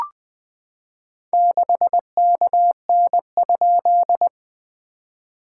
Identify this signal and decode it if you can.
Morse code, heard as a keyed tone, about 21 wpm, 690 Hz. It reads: 6KN?
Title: SOTA QSO Practice 20wpm